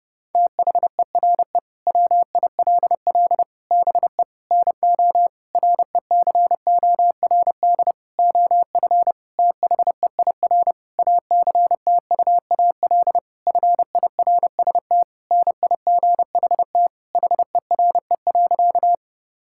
Morse code, heard as a keyed tone, about 32 wpm, 700 Hz. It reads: THERE WILL BE NO RECORD OF THEIR ACTUAL FIRST NIGHT HERE.
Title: Sentences from Top 1000 Words - Review Part 5 - 30wpm